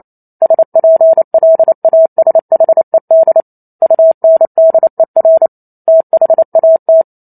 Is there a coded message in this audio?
Yes: Morse code from 0.4 to 7.0 s, heard as a keyed tone, about 28 wpm, 650 Hz. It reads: SPLASHED UNDER THAT